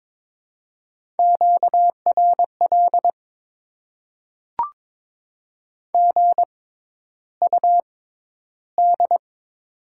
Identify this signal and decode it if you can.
Morse code, heard as a keyed tone, about 22 wpm, spaced as for 11 wpm, 700 Hz. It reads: QRL GUD
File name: Top 20 QSO Elements - 6x Character Spacing - 22wpm